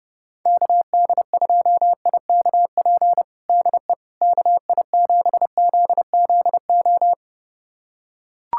Morse code, heard as a keyed tone, about 32 wpm, 700 Hz. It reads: KD2IKP DE KI7ZZO
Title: USA Callsign DE Callsign - 6 Characters - 30wpm